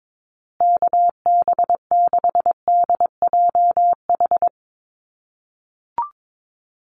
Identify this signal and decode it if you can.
Morse code, heard as a keyed tone, about 22 wpm, 700 Hz. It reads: KB6DJH
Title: USA Call Signs Repeated 3x 22wpm